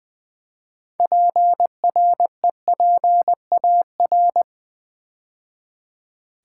Morse code, heard as a keyed tone, about 20 wpm, 700 Hz. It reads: PREPAR
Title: Build A Word - 3 Letters At A Time - 8-Letter Words - 20wpm